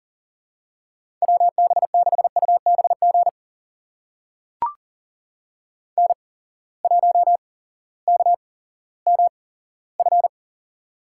Morse code, heard as a keyed tone, about 40 wpm, spaced as for 18 wpm, 700 Hz. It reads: WB6UBG N1KMF
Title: USA Call Signs - 8x Character Spacing - 40wpm